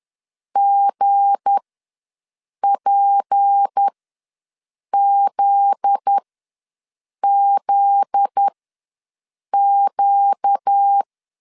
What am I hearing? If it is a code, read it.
Morse code, heard as a keyed tone, about 10 wpm, spaced as for 6 wpm, 800 Hz. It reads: GPZZQ
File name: LEZIONI / 7 GF9Z / 7 GF9Z ws 10wpm 3919